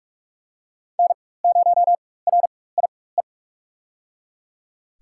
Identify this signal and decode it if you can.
Morse code, heard as a keyed tone, about 43 wpm, spaced as for 21 wpm, 700 Hz. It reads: N0RIE
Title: USA Call Signs - 4x Character Spacing - 45wpm